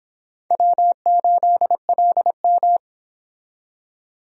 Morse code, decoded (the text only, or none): W8LM